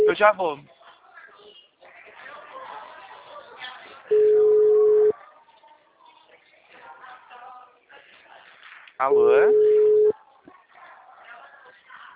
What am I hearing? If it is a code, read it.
Ringback tone (Europe, 425 Hz), with speech.
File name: out- -2002- - - .8